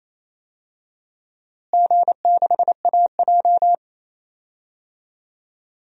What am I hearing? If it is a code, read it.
Morse code, heard as a keyed tone, about 27 wpm, 700 Hz. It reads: G6AJ
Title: DXCC-100 Call Signs - 4 Characters - Repeated 2x - 28wpm